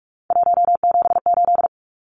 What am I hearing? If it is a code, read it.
Morse code, heard as a keyed tone, about 44 wpm, 700 Hz. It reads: 178
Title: Numbers - 3-Digit 45wpm